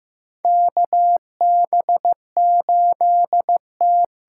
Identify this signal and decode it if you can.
Morse code, heard as a keyed tone, about 15 wpm, 700 Hz. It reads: KB8T
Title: USA Call Signs - 6 Characters - 6x Character Spacing - 15wpm